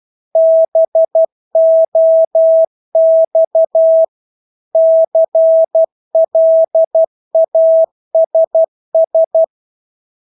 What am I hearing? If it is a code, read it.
Morse code, heard as a keyed tone, about 12 wpm, 650 Hz. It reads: BOX CLASS
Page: Engelske ord 12 wpm